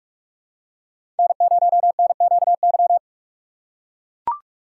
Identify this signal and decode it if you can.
Morse code, heard as a keyed tone, about 43 wpm, 700 Hz. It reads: N0NQY
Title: USA Call Signs - 5 Characters - 3x Character Spacing - 45wpm